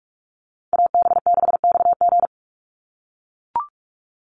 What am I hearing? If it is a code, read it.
Morse code, heard as a keyed tone, about 44 wpm, 700 Hz. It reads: AB6XG
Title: Call Signs - 45wpm